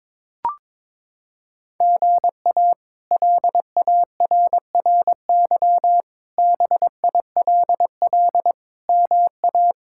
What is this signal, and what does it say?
Morse code, heard as a keyed tone, about 22 wpm, 700 Hz. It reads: GA LARRY BILL MA
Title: K1USN SST QSO Practice - 22wpm